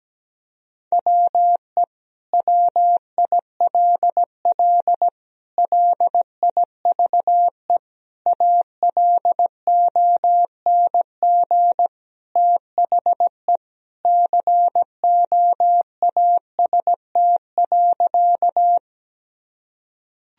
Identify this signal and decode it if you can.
Morse code, heard as a keyed tone, about 17 wpm, 700 Hz. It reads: WE WILL LIVE ALONG THE COAST.
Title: Sentences from Top 1000 Words - Review Part 4 - 17wpm